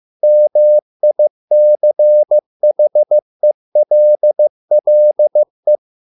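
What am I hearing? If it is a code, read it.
Morse code, heard as a keyed tone, about 15 wpm, 600 Hz. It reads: MICHELLE
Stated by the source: Morse code at a slow 15 wpm from CW med Gnister.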